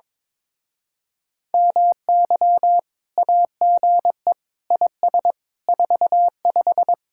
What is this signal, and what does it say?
Morse code, heard as a keyed tone, about 22 wpm, 700 Hz. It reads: MY AGE IS 45
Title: Speed-Racing Common QSO Phrases - 22wpm